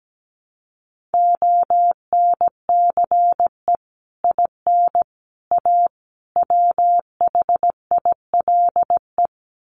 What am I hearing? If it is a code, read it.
Morse code, heard as a keyed tone, about 17 wpm, 700 Hz. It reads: ONCE IN A WHILE
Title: Sets of 4 Words 17wpm